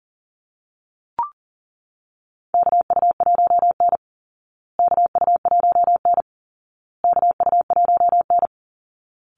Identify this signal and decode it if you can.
Morse code, heard as a keyed tone, about 40 wpm, 700 Hz. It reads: KU1N KU1N KU1N
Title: Call Signs Repeated 2x - 40wpm